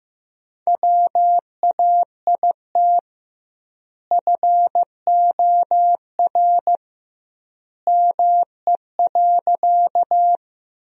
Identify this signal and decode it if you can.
Morse code, heard as a keyed tone, about 15 wpm, 700 Hz. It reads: WAIT FOR ME.
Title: 2x Word-Spacing - Sentences from Top 500 Words - 15wpm